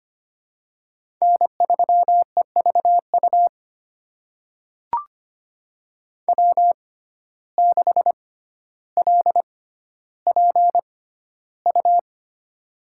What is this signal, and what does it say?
Morse code, heard as a keyed tone, about 25 wpm, spaced as for 13 wpm, 700 Hz. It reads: N3EVU W6LPU
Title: USA Call Signs - 5 Characters - 6x Character Spacing - 25wpm